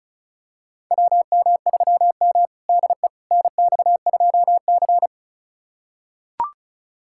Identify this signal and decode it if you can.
Morse code, heard as a keyed tone, about 36 wpm, 700 Hz. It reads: WM3M DE NX2C
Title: USA Callsign DE Callsign - 4 Characters - 35wpm